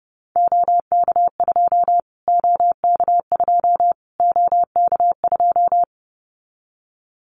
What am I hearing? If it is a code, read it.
Morse code, heard as a keyed tone, about 30 wpm, 700 Hz. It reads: OK2 OK2 OK2
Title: Callsign Prefixes for Top 25 DXCC - Repeated 3x - 30wpm